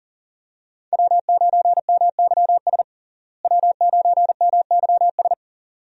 Morse code, heard as a keyed tone, about 40 wpm, 700 Hz. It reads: W9MYS W9MYS
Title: USA Call Signs - 5 Characters - Repeated 2x - 40wpm